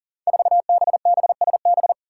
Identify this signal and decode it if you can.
Morse code, heard as a keyed tone, about 42 wpm, 700 Hz. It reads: 4BBSB
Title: Five Challenging Letter-Number Combinations 40wpm